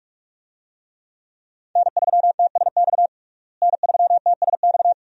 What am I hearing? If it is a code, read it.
Morse code, heard as a keyed tone, about 46 wpm, 700 Hz. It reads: N3TSX N3TSX
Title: USA Call Signs - 5 Characters - Repeated 2x - 45wpm